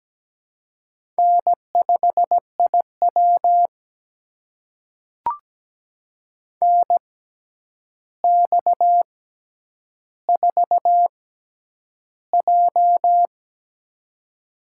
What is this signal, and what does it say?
Morse code, heard as a keyed tone, about 17 wpm, spaced as for 9 wpm, 700 Hz. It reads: N5IW NX4J
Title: USA Call Signs - 4 Characters - 6x Character Spacing - 17wpm